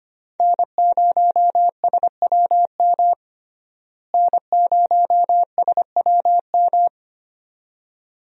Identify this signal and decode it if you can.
Morse code, heard as a keyed tone, about 25 wpm, 690 Hz. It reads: N0SWM N0SWM
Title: USA Call Signs - 5 Characters - Repeated 2x - 25wpm